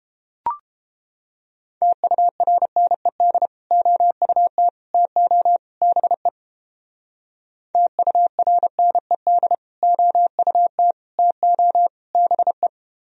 Morse code, heard as a keyed tone, about 29 wpm, 700 Hz. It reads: TURNED OUT TO BE TURNED OUT TO BE